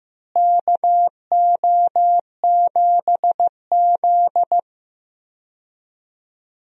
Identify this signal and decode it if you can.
Morse code, heard as a keyed tone, about 16 wpm, 700 Hz. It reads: KO7Z
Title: USA Call Signs - 4 Characters - Repeated 2x - 15wpm